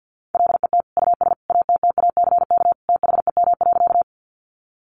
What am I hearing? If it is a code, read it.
Morse code, heard as a keyed tone, about 50 wpm, 700 Hz. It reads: LET US ATTACK THEM.
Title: Sentences from Top 1500 Words 50wpm